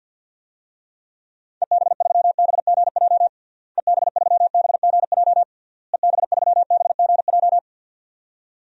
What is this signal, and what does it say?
Morse code, heard as a keyed tone, about 50 wpm, 700 Hz. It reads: EB3BGJ EB3BGJ EB3BGJ
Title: DXCC-100 Call Signs - 6 Characters - Repeated 3x - 50wpm